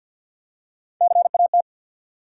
Morse code, heard as a keyed tone, about 47 wpm, 700 Hz. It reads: CAT
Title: Three Letter Words - 50wpm